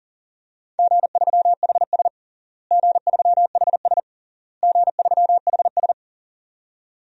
Morse code, heard as a keyed tone, about 40 wpm, 700 Hz. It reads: G3HS G3HS G3HS
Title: DXCC-100 Call Signs - 4 Characters - Repeated 3x - 40wpm